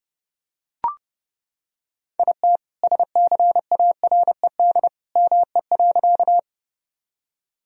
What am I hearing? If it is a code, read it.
Morse code, heard as a keyed tone, about 30 wpm, 700 Hz. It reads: IT SCARED ME.
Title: Sentences from Top 1000 Words - Review Part 4 - 30wpm